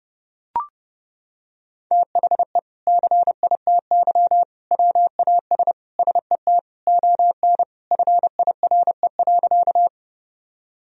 Morse code, heard as a keyed tone, about 30 wpm, 700 Hz. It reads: THE CITY WAS SET ON FIRE.
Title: Sentences from Top 1000 Words - Review Part 1 - 30wpm